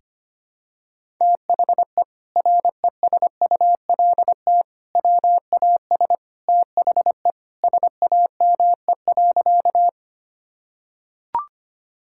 Morse code, heard as a keyed tone, about 25 wpm, 700 Hz. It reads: THE RESULT WAS THE SAME.
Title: Sentences from Top 1000 Words - Review Part 5 - 25wpm